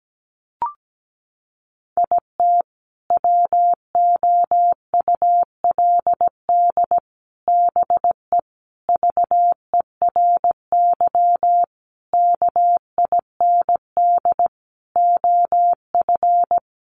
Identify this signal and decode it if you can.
Morse code, heard as a keyed tone, about 17 wpm, 700 Hz. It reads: IT WOULD BE VERY KIND OF
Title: Sentences from Top 200 Words 17wpm